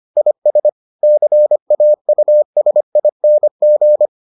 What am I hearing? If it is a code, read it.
Morse code, heard as a keyed tone, about 25 wpm, 600 Hz. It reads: IS CAUSING